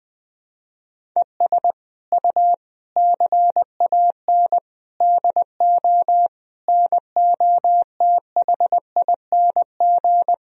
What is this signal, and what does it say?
Morse code, heard as a keyed tone, about 20 wpm, 700 Hz. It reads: ES U CAN DO NOTHING
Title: Sentences from Top 500 Words and Abbreviations 20wpm